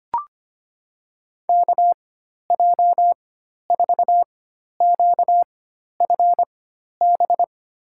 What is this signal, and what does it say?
Morse code, heard as a keyed tone, about 25 wpm, spaced as for 12 wpm, 700 Hz. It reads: KJ4QFB
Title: USA Call Signs - 6 Characters - 4x Character Spacing - 25wpm